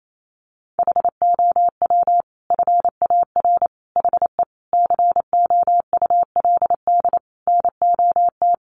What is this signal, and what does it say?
Morse code, heard as a keyed tone, about 28 wpm, 700 Hz. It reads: HOW FAR HE COULD NOT
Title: Sentences from Top 1500 Words 28wpm